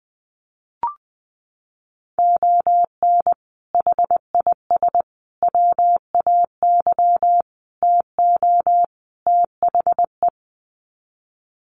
Morse code, heard as a keyed tone, about 20 wpm, 700 Hz. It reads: ON HIS WAY TO THE
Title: Sets of 5 Words Encore2 20wpm